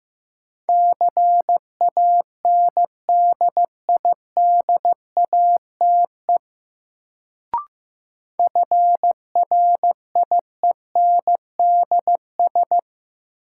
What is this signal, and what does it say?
Morse code, heard as a keyed tone, about 15 wpm, 700 Hz. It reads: CANDIDATE FRIENDS
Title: Sentences from Top 1600 Words - New Word Drill Alt - 15wpm